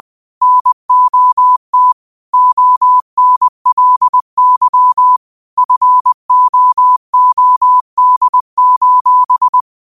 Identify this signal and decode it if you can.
Morse code, heard as a keyed tone, about 20 wpm, 1000 Hz. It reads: NOT ONLY FOOD:
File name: qotdfiles / 20wpm